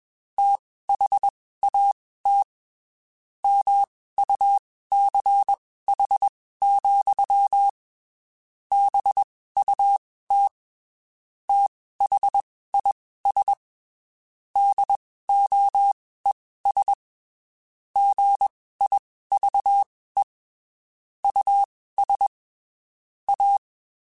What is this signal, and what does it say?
Morse code, heard as a keyed tone, about 21 wpm, spaced as for 15 wpm, 800 Hz. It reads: THAT MUCH, BUT THIS DOES GIVE US A
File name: (spaced) / QSOs Spaced-mp3s / WD0FBT @20 sp